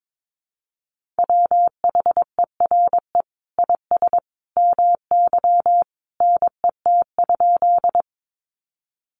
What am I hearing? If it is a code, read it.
Morse code, heard as a keyed tone, about 22 wpm, 700 Hz. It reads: WHERE IS MY NET?
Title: Sentences from Top 1400 Words 22wpm